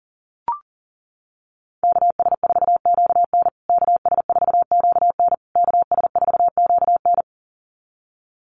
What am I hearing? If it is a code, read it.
Morse code, heard as a keyed tone, about 40 wpm, 700 Hz. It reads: KS4QN KS4QN KS4QN
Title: USA Call Signs Repeated 3x 40wpm